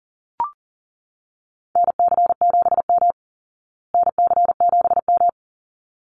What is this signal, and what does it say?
Morse code, heard as a keyed tone, about 40 wpm, 700 Hz. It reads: NC7M NC7M
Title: Call Signs Repeated 3x - 40wpm